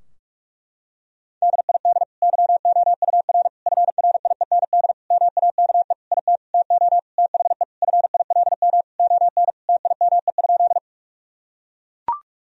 Morse code, heard as a keyed tone, about 41 wpm, 700 Hz. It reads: DID YOUR FRIEND MAKE IT TO THE FILM ON TIME?
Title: Sentences from Top 1000 Words - Review Part 3 - 45wpm